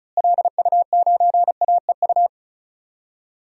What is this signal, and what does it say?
Morse code, heard as a keyed tone, about 36 wpm, 700 Hz. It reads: LU9AEU
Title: DXCC-100 Call Signs - 6 Characters - Repeated 2x - 35wpm